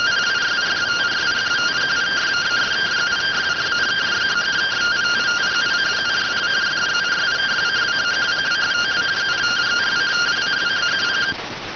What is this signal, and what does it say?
Two-tone teleprinter signal, about 170 Hz shift.